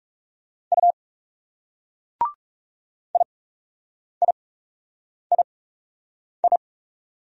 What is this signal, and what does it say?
Morse code, heard as a keyed tone, about 37 wpm, spaced as for 8 wpm, 700 Hz. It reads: U IIII
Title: Speed-Racing Single Letters 30wpm